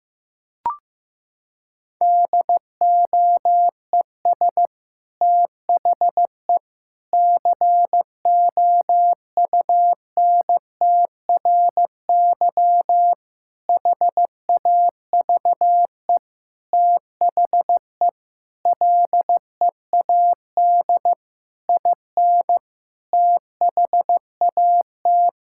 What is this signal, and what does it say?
Morse code, heard as a keyed tone, about 15 wpm, 700 Hz. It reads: DOES THE COUNTRY HAVE THE LEAD IN THAT
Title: Sentences from Top 1000 Words - Review Part 5 - 15wpm